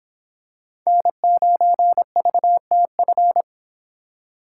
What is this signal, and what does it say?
Morse code, heard as a keyed tone, about 24 wpm, 700 Hz. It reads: N9VTF